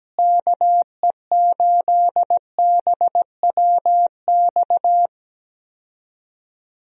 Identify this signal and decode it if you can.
Morse code, heard as a keyed tone, about 17 wpm, 700 Hz. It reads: KE8BWX
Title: USA Call Signs - 6 Characters - Repeated 3x - 17wpm